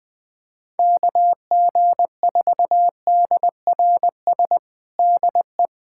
Morse code, heard as a keyed tone, about 19 wpm, 700 Hz. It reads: KG4DRS DE
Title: USA Callsign DE Callsign - 6 Characters - 20wpm